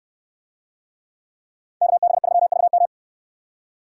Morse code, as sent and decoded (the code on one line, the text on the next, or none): -.- -... ...-- ...- --
KB3VM